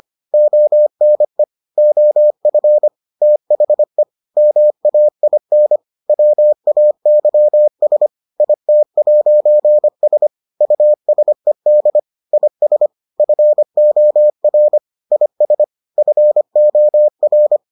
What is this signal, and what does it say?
Morse code, heard as a keyed tone, about 25 wpm, 600 Hz. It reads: ONE OF THE MAIN WAYS IT'S USED IS FOR IS FOR